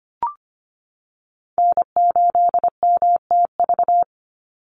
Morse code, heard as a keyed tone, about 25 wpm, 700 Hz. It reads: N8MTV
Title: Call Signs - 25wpm